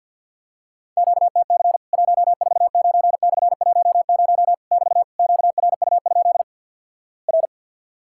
Morse code, heard as a keyed tone, about 50 wpm, 700 Hz. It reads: QTC 149/10 <BT> QRU? R
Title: Worked All Europe DX Contest - QTC Exchange with Repeats - 50wpm